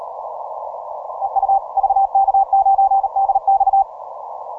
Morse code, heard as a keyed tone, about 39 wpm, 780 Hz. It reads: EU VK9DX